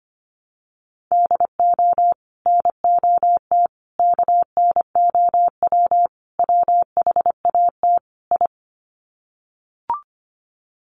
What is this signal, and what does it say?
Morse code, heard as a keyed tone, about 25 wpm, 700 Hz. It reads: DO NOT KNOW WHAT I
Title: Sets of 5 Words 25wpm